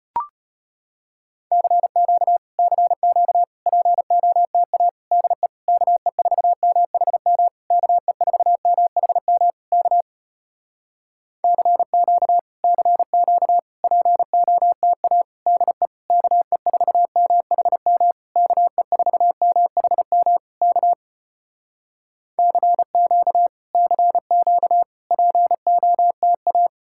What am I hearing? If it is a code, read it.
Morse code, heard as a keyed tone, about 35 wpm, 700 Hz. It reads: CQ CQ POTA DE KE4MHM KE4MHM K CQ CQ POTA DE KE4MHM KE4MHM K CQ CQ POTA